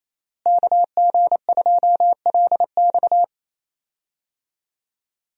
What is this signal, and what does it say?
Morse code, heard as a keyed tone, about 27 wpm, 700 Hz. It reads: KG2LX
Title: USA Call Signs - 5 Characters - Repeated 3x - 28wpm